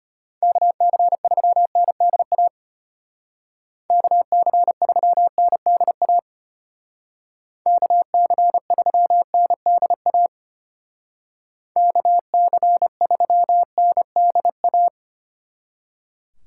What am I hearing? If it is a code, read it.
Morse code, heard as a keyed tone, about 31 wpm, 700 Hz. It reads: KC3NDA KC3NDA KC3NDA KC3NDA
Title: Speed Racing - USA 2x3 Call Signs 25wpm